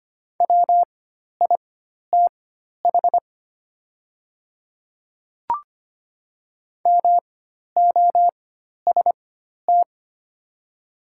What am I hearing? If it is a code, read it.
Morse code, heard as a keyed tone, about 25 wpm, spaced as for 12 wpm, 700 Hz. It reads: WITH MOST
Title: 4x Character Spacing - Top 100 Words - 25wpm